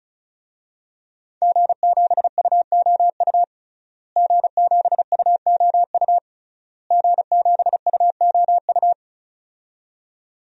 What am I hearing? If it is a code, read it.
Morse code, heard as a keyed tone, about 36 wpm, 700 Hz. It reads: G7UOU G7UOU G7UOU
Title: DXCC-100 Call Signs - 5 Characters - Repeated 3x - 35wpm